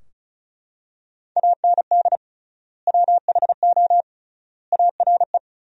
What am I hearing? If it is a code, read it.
Morse code, heard as a keyed tone, about 36 wpm, 700 Hz. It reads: AND WHO ARE
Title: 3x Word-Spacing - Sentences from Top 100 Words - 35wpm